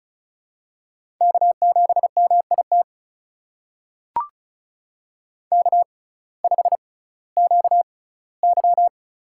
Morse code, heard as a keyed tone, about 36 wpm, spaced as for 19 wpm, 700 Hz. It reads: K7MIT K5QY
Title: USA Call Signs - 6x Character Spacing - 35wpm